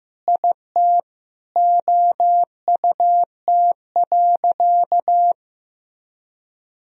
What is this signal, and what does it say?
Morse code, heard as a keyed tone, about 15 wpm, 700 Hz. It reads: IT OUT.
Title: Sentences from Top 1000 Words - Review Part 3 - 15wpm